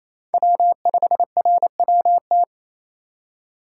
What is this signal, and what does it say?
Morse code, heard as a keyed tone, about 28 wpm, 700 Hz. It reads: W5RWT
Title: USA Call Signs - 5 Characters - 4x Character Spacing - 28wpm